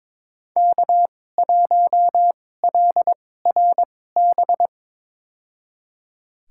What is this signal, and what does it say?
Morse code, heard as a keyed tone, about 22 wpm, spaced as for 16 wpm, 700 Hz. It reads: K1LRB
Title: USA Call Signs - 5 Characters - 2x Character Spacing - 22wpm